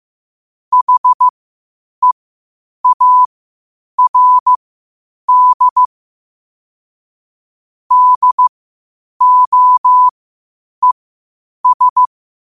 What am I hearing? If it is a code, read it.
Morse code, heard as a keyed tone, about 15 wpm, spaced as for 8 wpm, 1000 Hz. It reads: HEARD DOES
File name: qotdfiles / 8wpm